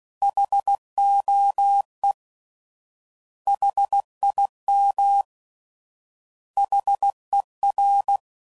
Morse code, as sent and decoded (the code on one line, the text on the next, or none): .... --- . / .... .. -- / .... .-.
HOE HIM HER